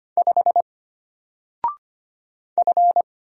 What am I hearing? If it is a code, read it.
Morse code, heard as a keyed tone, about 25 wpm, 700 Hz. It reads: 5 F
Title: Single Letter-Number Alt 25wpm